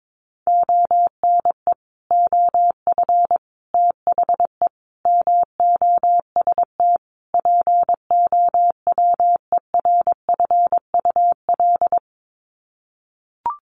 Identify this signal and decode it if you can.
Morse code, heard as a keyed tone, about 22 wpm, 700 Hz. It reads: ONE OF THE MOST POWERFUL
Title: Sets of 5 Words Encore 22wpm